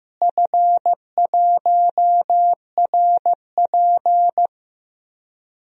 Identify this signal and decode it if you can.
Morse code, heard as a keyed tone, about 15 wpm, 700 Hz. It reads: F1RP
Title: DXCC-100 Call Signs - 4 Characters - Repeated 3x - 15wpm